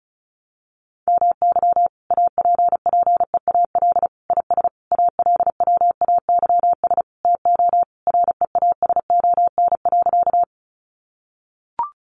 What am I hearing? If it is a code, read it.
Morse code, heard as a keyed tone, about 34 wpm, 700 Hz. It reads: MY APPEAL IS ALWAYS TO REASON.
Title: Sentences from Top 1300 Words 35wpm